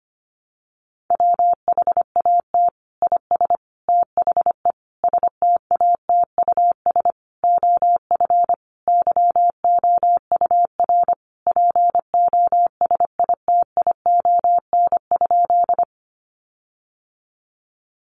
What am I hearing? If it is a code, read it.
Morse code, heard as a keyed tone, about 25 wpm, 700 Hz. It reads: WHAT IS THE STATUS OF YOUR POSITION?
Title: Sentences from Top 1400 Words 25wpm